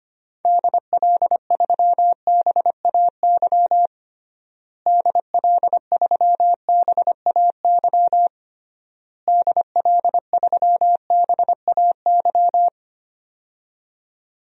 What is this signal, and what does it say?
Morse code, heard as a keyed tone, about 25 wpm, 700 Hz. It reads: DL3BAY DL3BAY DL3BAY
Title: DXCC-100 Call Signs - 6 Characters - Repeated 3x - 25wpm